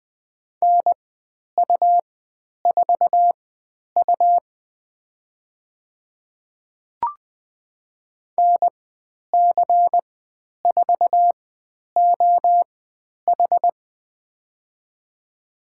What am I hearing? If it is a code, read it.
Morse code, heard as a keyed tone, about 20 wpm, spaced as for 10 wpm, 700 Hz. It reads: NU4U NC4OH
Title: 102 Lesson 17g - Letter u - Call Signs 20wpm